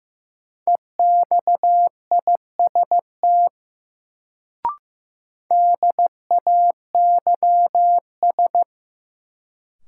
Morse code, heard as a keyed tone, about 15 wpm, 700 Hz. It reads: EXIST DAYS
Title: Sentences from Top 1000 Words - Review Part 5 - New Word Drill - 15wpm